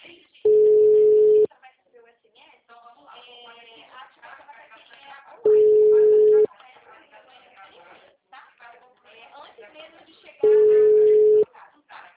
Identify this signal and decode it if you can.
Ringback tone (Europe, 425 Hz), with speech.